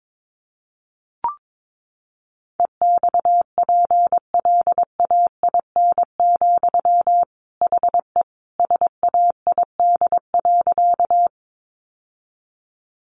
Morse code, heard as a keyed tone, about 22 wpm, 700 Hz. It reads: EXPLAIN, HE SAID.
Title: Sentences from Top 1100 Words 22wpm